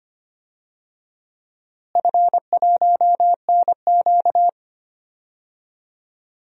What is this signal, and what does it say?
Morse code, heard as a keyed tone, about 25 wpm, 700 Hz. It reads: F1NQ
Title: DXCC-100 Call Signs - 4 Characters - Repeated 2x - 25wpm